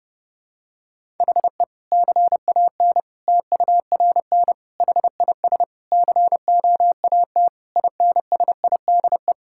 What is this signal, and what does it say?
Morse code, heard as a keyed tone, about 30 wpm, 700 Hz. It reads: HE CAN TURN HIS COAT INSIDE
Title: Sentences from Top 1600 Words 30wpm